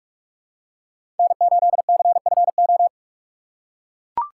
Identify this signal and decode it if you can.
Morse code, heard as a keyed tone, about 44 wpm, 690 Hz. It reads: N8CFO
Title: USA Call Signs - 5 Characters - 6x Character Spacing - 45wpm